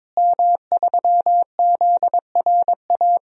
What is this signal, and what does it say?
Morse code, heard as a keyed tone, about 22 wpm, 700 Hz. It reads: M3ZRA
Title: DXCC-100 Call Signs - 5 Characters - Repeated 2x - 22wpm